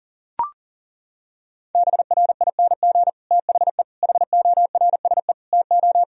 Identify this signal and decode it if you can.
Morse code, heard as a keyed tone, about 40 wpm, 700 Hz. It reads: BRING THE HORSE TO
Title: Sentences from Top 1000 Words - Review Part 3 - 40wpm